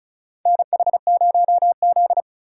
Morse code, heard as a keyed tone, about 36 wpm, 700 Hz. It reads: NH0Z
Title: USA Call Signs - 4 Characters - 8x Character Spacing - 35wpm